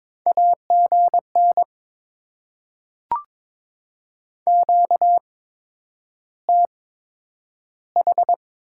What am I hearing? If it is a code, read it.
Morse code, heard as a keyed tone, about 22 wpm, spaced as for 9 wpm, 700 Hz. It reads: AGN QTH
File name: Top 20 QSO Elements - 8x Character Spacing - 22wpm